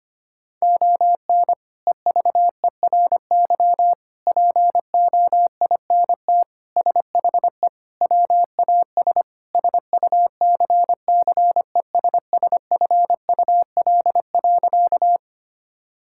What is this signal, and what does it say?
Morse code, heard as a keyed tone, about 25 wpm, 700 Hz. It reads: ON EVERY POINT SHE WAS SUCCESSFUL.
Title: Sentences from Top 1000 Words - Review Part 6 - 25wpm